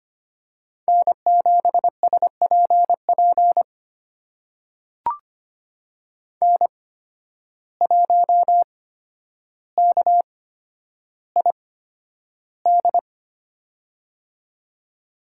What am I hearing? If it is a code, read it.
Morse code, heard as a keyed tone, about 25 wpm, spaced as for 11 wpm, 700 Hz. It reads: N7SPP N1KID
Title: USA Call Signs - 5 Characters - 8x Character Spacing - 25wpm